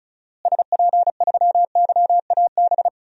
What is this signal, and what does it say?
Morse code, heard as a keyed tone, about 35 wpm, 700 Hz. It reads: SP3YAB
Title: DXCC-100 Call Signs - 6 Characters - Repeated 2x - 35wpm